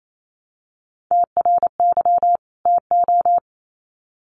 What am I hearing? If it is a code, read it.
Morse code, heard as a keyed tone, about 28 wpm, 700 Hz. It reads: TRY TO
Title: Sets of 2 Words 28wpm